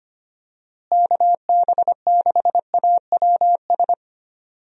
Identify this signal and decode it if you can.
Morse code, heard as a keyed tone, about 25 wpm, 700 Hz. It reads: KB6AWS
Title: USA Call Signs - 6 Characters - Repeated 3x - 25wpm